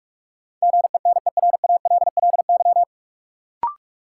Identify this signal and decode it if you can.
Morse code, heard as a keyed tone, about 46 wpm, 700 Hz. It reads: GENERALLY
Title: Sentences from Top 600 Words - New Word Drill - 45wpm